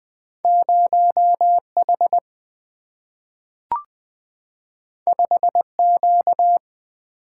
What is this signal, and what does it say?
Morse code, heard as a keyed tone, about 20 wpm, 700 Hz. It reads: 0H 5Q
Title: Single-Letter Single-Number 20wpm